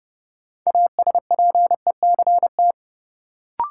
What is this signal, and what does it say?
Morse code, heard as a keyed tone, about 30 wpm, 700 Hz. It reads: ASPECT
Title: Sentences from Top 1300 Words - New Word Drill - 30wpm